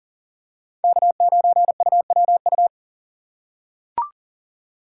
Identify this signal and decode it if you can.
Morse code, heard as a keyed tone, about 40 wpm, 700 Hz. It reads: K9UWU
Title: USA Call Signs - 5 Characters - 3x Character Spacing - 40wpm